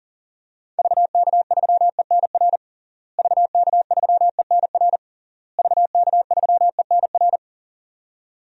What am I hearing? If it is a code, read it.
Morse code, heard as a keyed tone, about 40 wpm, 700 Hz. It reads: VK3ENR VK3ENR VK3ENR
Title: DXCC-100 Call Signs - 6 Characters - Repeated 3x - 40wpm